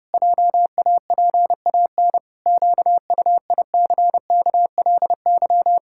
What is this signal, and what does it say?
Morse code, heard as a keyed tone, about 32 wpm, 700 Hz. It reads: JAPAN QUICKLY
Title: Pangram Challenge v2 - 30wpm